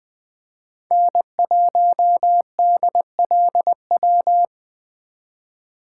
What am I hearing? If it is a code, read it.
Morse code, heard as a keyed tone, about 23 wpm, 700 Hz. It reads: N1DLW
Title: Speed Racing - USA 1x3 Call Signs 20wpm